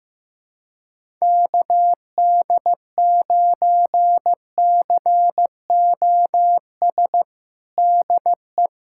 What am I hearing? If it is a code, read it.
Morse code, heard as a keyed tone, about 15 wpm, 700 Hz. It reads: KD9COS DE
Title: USA Callsign DE Callsign - 6 Characters - 15wpm